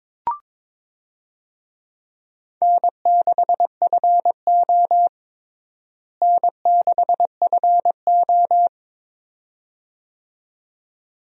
Morse code, heard as a keyed tone, about 22 wpm, 700 Hz. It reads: N6FO N6FO
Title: USA Call Signs - 4 Characters - Repeated 2x - 22wpm